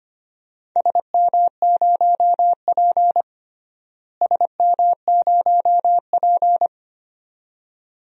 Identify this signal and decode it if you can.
Morse code, heard as a keyed tone, about 25 wpm, 700 Hz. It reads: SM0P SM0P
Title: DXCC-100 Call Signs - 4 Characters - Repeated 2x - 25wpm